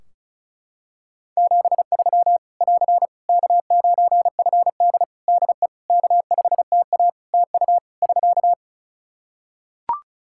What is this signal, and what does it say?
Morse code, heard as a keyed tone, about 34 wpm, 700 Hz. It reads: 73 <AR> K9FD DE K5TA TU <SK>
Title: K1USN SST QSO Practice 35wpm